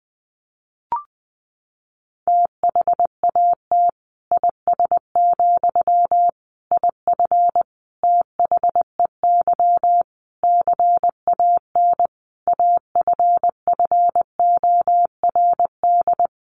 Morse code, heard as a keyed tone, about 20 wpm, 700 Hz. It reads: THAT IS, IF THEY CAN AFFORD
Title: Sentences from Top 1100 Words 20wpm